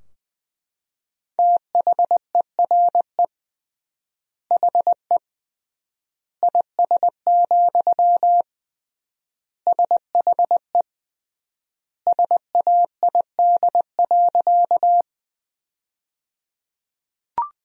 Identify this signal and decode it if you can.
Morse code, heard as a keyed tone, about 20 wpm, 700 Hz. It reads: THERE HE IS, SHE SAID.
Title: 3x Word-Spacing - Sentences from Top 100 Words - 20wpm